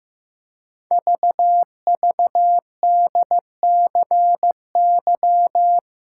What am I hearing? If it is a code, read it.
Morse code, heard as a keyed tone, about 15 wpm, 700 Hz. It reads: VVDCY
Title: Code Groups - 5 Letters - 15wpm